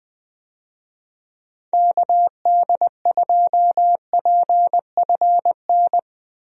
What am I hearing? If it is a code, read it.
Morse code, heard as a keyed tone, about 20 wpm, 700 Hz. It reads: KD2PFN
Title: USA Call Signs - 6 Characters - 20wpm